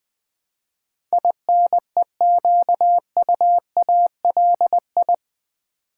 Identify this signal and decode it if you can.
Morse code, heard as a keyed tone, about 20 wpm, 700 Hz. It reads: INEQUALI
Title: Build A Word - 4 Letters At A Time - 10-Letter Words - 20wpm